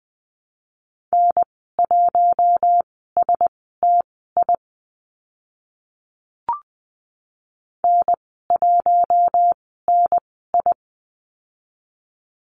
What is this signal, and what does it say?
Morse code, heard as a keyed tone, about 20 wpm, spaced as for 14 wpm, 700 Hz. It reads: N1STI N1NI
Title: 042 Lesson 8g - Number 1 - Call Signs 20wpm